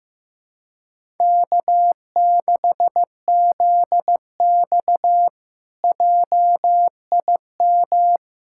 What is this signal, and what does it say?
Morse code, heard as a keyed tone, about 15 wpm, 700 Hz. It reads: K6ZX JIM
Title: K1USN SST QSO Practice - 15wpm